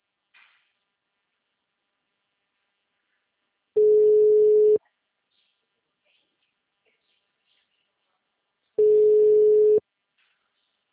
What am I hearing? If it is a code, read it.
Ringback tone (Europe, 425 Hz).
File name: out- -2006- - - .7480